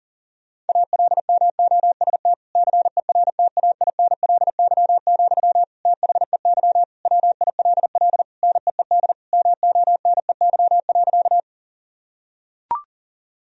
Morse code, heard as a keyed tone, about 40 wpm, 700 Hz. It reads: ALMOST CERTAINLY, THEY WILL NEED MONEY.
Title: Sentences from Top 1000 Words - Review Part 4 - 40wpm